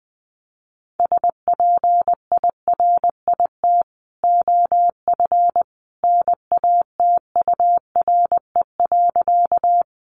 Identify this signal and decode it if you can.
Morse code, heard as a keyed tone, about 19 wpm, 700 Hz. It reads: SPIRIT OF NATURE.
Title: Sentences from Top 1400 Words 20wpm